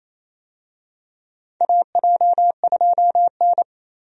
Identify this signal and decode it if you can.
Morse code, heard as a keyed tone, about 28 wpm, 700 Hz. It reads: AJ2N
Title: USA Call Signs - 4 Characters - 28wpm